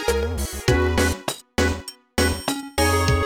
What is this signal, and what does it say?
There is no code.